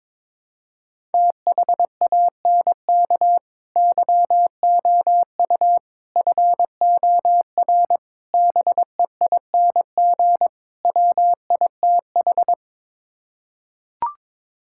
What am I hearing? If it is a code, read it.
Morse code, heard as a keyed tone, about 22 wpm, 700 Hz. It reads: THANK YOU FOR BEING WITH
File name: practice / 22wpm / Build A Phrase - 5-Word Phrases - 22wpm